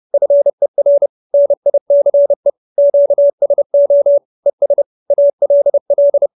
Morse code, heard as a keyed tone, about 30 wpm, 570 Hz. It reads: FER NICE QSO ES ALL